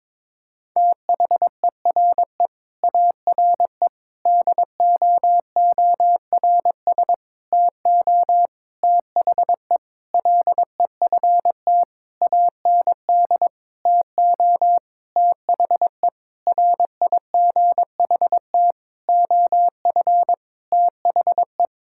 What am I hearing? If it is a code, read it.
Morse code, heard as a keyed tone, about 22 wpm, 700 Hz. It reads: THERE ARE DOORS TO THE LEFT AND TO THE RIGHT OF THE
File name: practice / 22wpm / Sentences with 8-Word Phrases - 22wpm